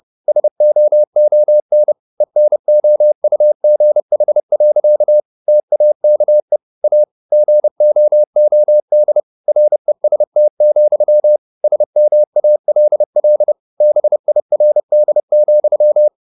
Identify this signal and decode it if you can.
Morse code, heard as a keyed tone, about 30 wpm, 600 Hz. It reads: SOON ENOUGH. TAKE A GOOD REST, SMALL BIRD,
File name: the old man and the sea / the old man and the sea 1 30wpm